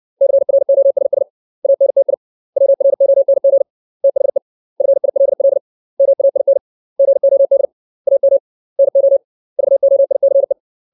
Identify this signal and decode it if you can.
Morse code, heard as a keyed tone, about 60 wpm, 550 Hz. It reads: CROSS ANTI WRONG THE FILL MAIN GOD AM NO VOICE
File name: 632 words / 632 Words 60WPM